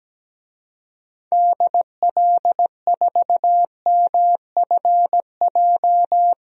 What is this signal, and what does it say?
Morse code, heard as a keyed tone, about 17 wpm, 700 Hz. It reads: DL4MFJ
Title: DXCC-100 Call Signs - 6 Characters - Repeated 3x - 17wpm